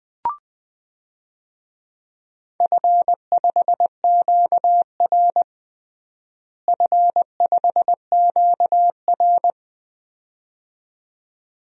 Morse code, heard as a keyed tone, about 20 wpm, 700 Hz. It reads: F5QR F5QR
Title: DXCC-100 Call Signs - 4 Characters - Repeated 2x - 20wpm